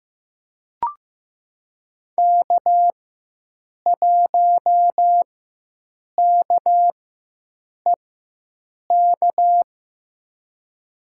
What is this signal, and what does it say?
Morse code, heard as a keyed tone, about 15 wpm, spaced as for 7 wpm, 700 Hz. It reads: K1KEK
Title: USA Call Signs - 5 Characters - 4x Character Spacing - 15wpm